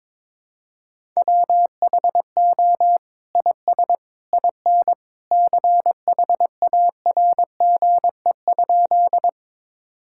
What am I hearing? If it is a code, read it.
Morse code, heard as a keyed tone, about 22 wpm, 700 Hz. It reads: WHO IS IN CHARGE?
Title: Sentences from Top 1000 Words - Review Part 6 - 22wpm